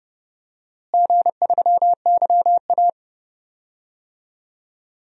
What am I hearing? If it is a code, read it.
Morse code, heard as a keyed tone, about 30 wpm, 700 Hz. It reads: G3YA
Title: DXCC-100 Call Signs - 4 Characters - Repeated 3x - 30wpm